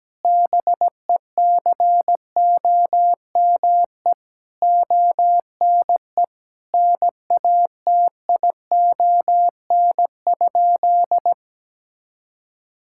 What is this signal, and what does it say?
Morse code, heard as a keyed tone, about 17 wpm, 700 Hz. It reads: BECOME ONE NATION?
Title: Sentences from Top 1000 Words - Review Part 6 - 17wpm